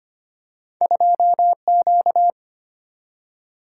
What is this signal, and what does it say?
Morse code, heard as a keyed tone, about 25 wpm, 700 Hz. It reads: 2Q
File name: Two Letter-Number Combinations - 25wpm